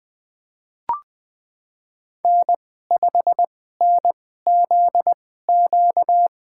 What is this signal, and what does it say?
Morse code, heard as a keyed tone, about 20 wpm, spaced as for 14 wpm, 700 Hz. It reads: N5NZQ
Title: USA Call Signs - 5 Characters - 2x Character Spacing - 20wpm